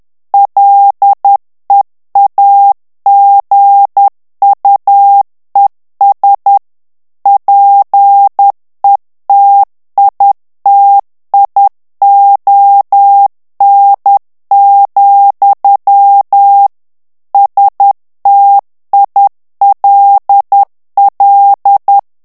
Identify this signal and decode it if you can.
Morse code, heard as a keyed tone, about 10 wpm, 800 Hz. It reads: LEAGUES PETITION, STILL
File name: ANGLAIS texte clair 3 vitesse 10 mots minute